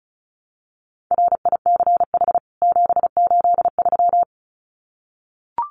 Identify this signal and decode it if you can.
Morse code, heard as a keyed tone, about 34 wpm, 700 Hz. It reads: RICH 783
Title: CWT Exchange 35wpm